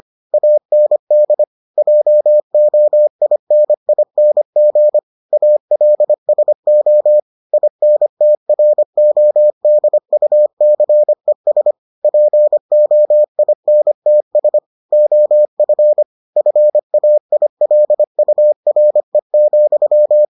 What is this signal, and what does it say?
Morse code, heard as a keyed tone, about 25 wpm, 600 Hz. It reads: AND JOINING ALSO INTRODUCES POINTS OF FAILURE,